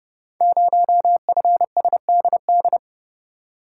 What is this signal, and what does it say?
Morse code, heard as a keyed tone, about 30 wpm, 700 Hz. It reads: 0FSDD
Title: Code Groups - 5 Letter:Numbers - 30wpm